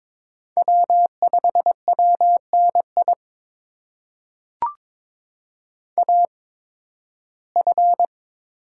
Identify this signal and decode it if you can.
Morse code, heard as a keyed tone, about 22 wpm, spaced as for 14 wpm, 700 Hz. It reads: W5WNI AF
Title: USA Call Signs - 5 Characters - 8x Character Spacing - 22wpm